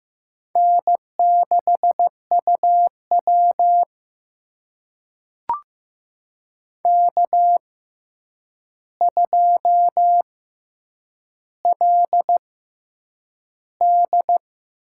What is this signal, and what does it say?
Morse code, heard as a keyed tone, about 15 wpm, spaced as for 8 wpm, 700 Hz. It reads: N6UW K2LD
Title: USA Call Signs - 4 Characters - 6x Character Spacing - 15wpm